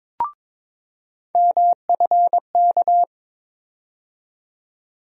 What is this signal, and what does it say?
Morse code, heard as a keyed tone, about 22 wpm, 700 Hz. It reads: MFK
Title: USA Call Sign Postfixes - 3 Letters 22wpm